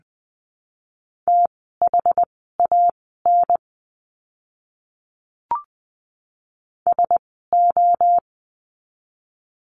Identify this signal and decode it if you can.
Morse code, heard as a keyed tone, about 20 wpm, spaced as for 15 wpm, 700 Hz. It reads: THAN SO
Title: 244 Lesson 39f - Letter z - Words 20wpm